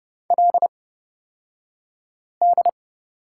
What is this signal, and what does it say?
Morse code, heard as a keyed tone, about 30 wpm, spaced as for 5 wpm, 700 Hz. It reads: LD
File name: Lesson 37d - Letter x - Two Characters 30wpm